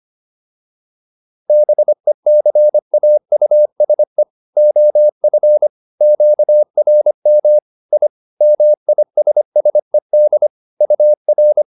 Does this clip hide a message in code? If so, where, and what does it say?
Yes: Morse code from 1.5 to 11.6 s, heard as a keyed tone, about 25 wpm, 600 Hz. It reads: BECAUSE OF QRM I MISSED UR